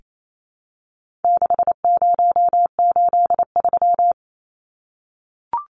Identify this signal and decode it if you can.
Morse code, heard as a keyed tone, about 28 wpm, 700 Hz. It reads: 6083
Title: Numbers - 4-Digit 28wpm